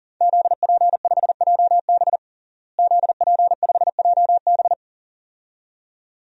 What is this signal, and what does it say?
Morse code, heard as a keyed tone, about 40 wpm, 700 Hz. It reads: ZP5JB ZP5JB
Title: DXCC-100 Call Signs - 5 Characters - Repeated 2x - 40wpm